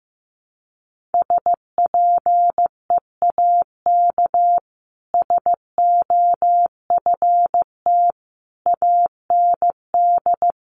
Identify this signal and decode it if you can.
Morse code, heard as a keyed tone, about 15 wpm, 700 Hz. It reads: SPEAK SOFT AND
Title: Sentences from Top 1500 Words 15wpm